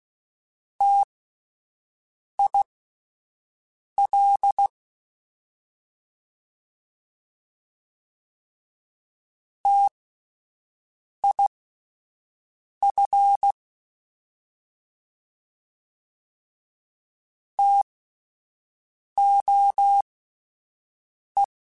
Morse code, heard as a keyed tone, about 16 wpm, spaced as for 5 wpm, 800 Hz. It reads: TIL TIF TOE